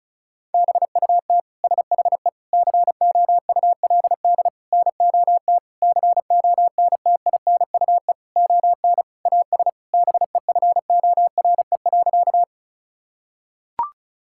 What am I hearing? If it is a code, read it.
Morse code, heard as a keyed tone, about 34 wpm, 700 Hz. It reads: BUT SHE COULD NOT CONTINUE ON AS BEFORE.
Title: Sentences from Top 1000 Words - Review Part 6 - 35wpm